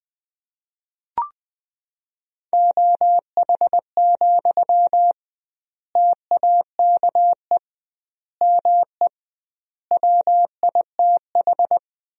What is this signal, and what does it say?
Morse code, heard as a keyed tone, about 21 wpm, 700 Hz. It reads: OH, TAKE ME WITH
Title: 2x Word-Spacing - Sentences from Top 500 Words - 20wpm